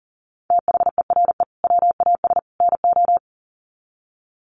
Morse code, heard as a keyed tone, about 40 wpm, 700 Hz. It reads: THERE WAS NO
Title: Sets of 3 Words 40wpm